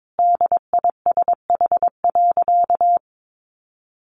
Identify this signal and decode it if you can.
Morse code, heard as a keyed tone, about 22 wpm, 700 Hz. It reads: DISH.
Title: Sentences from Top 1300 Words 22wpm